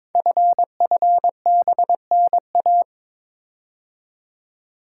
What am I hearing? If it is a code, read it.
Morse code, heard as a keyed tone, about 22 wpm, 700 Hz. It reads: FFBNA
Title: Code Groups - 5 Letters - 22wpm